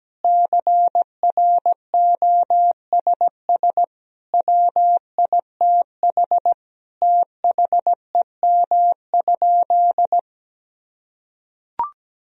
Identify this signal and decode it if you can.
Morse code, heard as a keyed tone, about 17 wpm, 700 Hz. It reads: CROSS WITH THEM?
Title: Sentences from Top 1000 Words - Review Part 1 - 17wpm